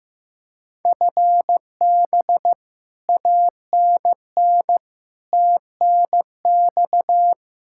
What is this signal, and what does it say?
Morse code, heard as a keyed tone, about 16 wpm, 700 Hz. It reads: FB ANN TNX
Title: Practice QSO - Exchange 3 - More Info - 15wpm